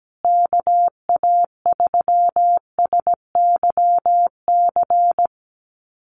Morse code, heard as a keyed tone, about 17 wpm, 700 Hz. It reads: KA3SYC
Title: USA Call Signs Repeated 2x 17wpm